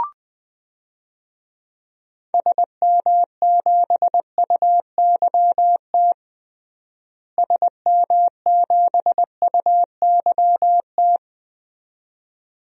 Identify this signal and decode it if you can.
Morse code, heard as a keyed tone, about 20 wpm, 700 Hz. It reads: SM7UYT SM7UYT
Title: DXCC-100 Call Signs - 6 Characters - Repeated 2x - 20wpm